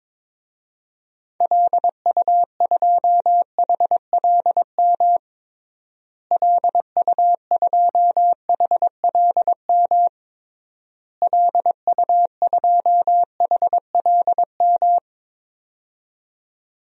Morse code, heard as a keyed tone, about 22 wpm, 700 Hz. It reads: LU2HLM LU2HLM LU2HLM
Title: DXCC-100 Call Signs - 6 Characters - Repeated 3x - 22wpm